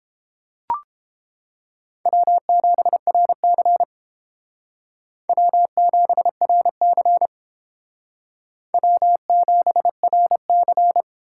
Morse code, heard as a keyed tone, about 28 wpm, 700 Hz. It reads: W7RC W7RC W7RC